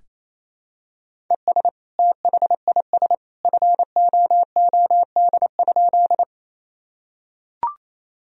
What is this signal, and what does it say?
Morse code, heard as a keyed tone, about 28 wpm, 700 Hz. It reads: ES THIS FOOD?
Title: Sentences from Top 300 Words and Abbreviations 28wpm